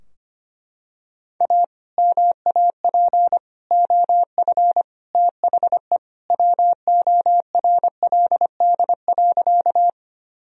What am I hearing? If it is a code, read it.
Morse code, heard as a keyed tone, about 25 wpm, 700 Hz. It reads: A MAP OF THE WORLD.
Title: Sentences from Top 1000 Words - Review Part 2 - 25wpm